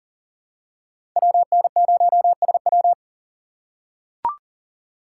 Morse code, heard as a keyed tone, about 40 wpm, 700 Hz. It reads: WN0SW
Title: USA Call Signs - 5 Characters - 6x Character Spacing - 40wpm